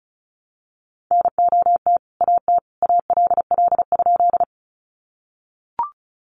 Morse code, heard as a keyed tone, about 34 wpm, 700 Hz. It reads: NOT AT ALL?
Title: Sentences from Top 600 Words 35wpm